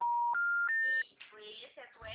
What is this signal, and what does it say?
Special information tone, with speech.
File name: out- -2006